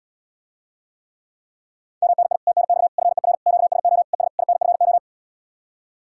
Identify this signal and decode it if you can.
Morse code, heard as a keyed tone, about 80 wpm, 700 Hz. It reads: MNI TNX FER QSO ES INFO